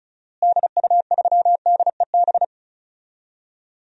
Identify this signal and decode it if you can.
Morse code, heard as a keyed tone, about 36 wpm, 700 Hz. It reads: DU3DEB
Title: DXCC-100 Call Signs - 6 Characters - Repeated 3x - 35wpm